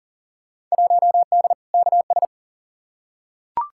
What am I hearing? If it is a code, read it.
Morse code, heard as a keyed tone, about 40 wpm, 700 Hz. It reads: 1D KS
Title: Field Day Exchanges - 40wpm